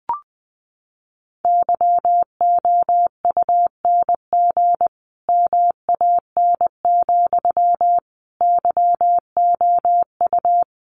Morse code, heard as a keyed tone, about 20 wpm, 700 Hz. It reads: YOUNG MAN, YOU
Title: Sentences from Top 400 Words 20wpm